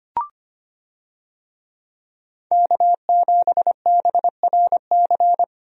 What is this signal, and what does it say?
Morse code, heard as a keyed tone, about 25 wpm, 700 Hz. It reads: K7BRC
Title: USA Call Signs - 5 Characters - 25wpm